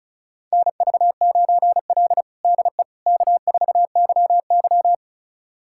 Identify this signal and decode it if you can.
Morse code, heard as a keyed tone, about 36 wpm, 700 Hz. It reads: NV9L DE K4YY
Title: USA Callsign DE Callsign - 4 Characters - 35wpm